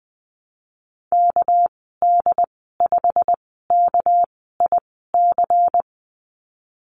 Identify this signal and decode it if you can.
Morse code, heard as a keyed tone, about 20 wpm, spaced as for 15 wpm, 700 Hz. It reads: KD5KIC
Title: 207 Lesson 33g - Letter j - Call Signs 20wpm